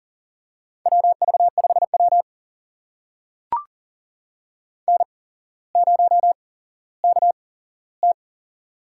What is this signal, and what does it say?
Morse code, heard as a keyed tone, about 38 wpm, spaced as for 17 wpm, 700 Hz. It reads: WV5W N0KT
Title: USA Call Signs - 4 Characters - 8x Character Spacing - 40wpm